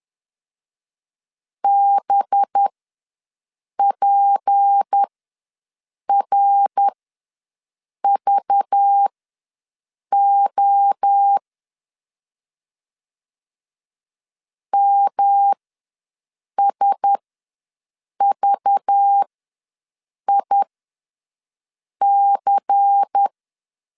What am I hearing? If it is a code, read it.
Morse code, heard as a keyed tone, about 10 wpm, spaced as for 6 wpm, 800 Hz. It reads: BPRVO MSVIC
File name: LEZIONI / 6 PKX0 / 6 P ws 10wpm 3751